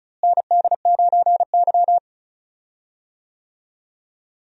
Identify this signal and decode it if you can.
Morse code, heard as a keyed tone, about 36 wpm, 700 Hz. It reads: ND9Y
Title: USA Call Signs - 4 Characters - Repeated 3x - 35wpm